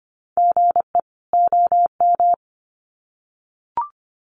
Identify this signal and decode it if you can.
Morse code, heard as a keyed tone, about 25 wpm, 700 Hz. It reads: GE OM
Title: Common QSO Phrases 25wpm